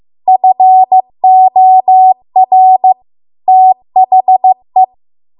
Morse code, heard as a keyed tone, about 15 wpm, 750 Hz. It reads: FOR THE